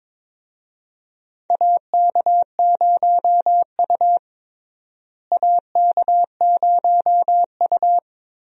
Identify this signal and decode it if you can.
Morse code, heard as a keyed tone, about 22 wpm, 700 Hz. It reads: AK0U AK0U
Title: USA Call Signs - 4 Characters - Repeated 3x - 22wpm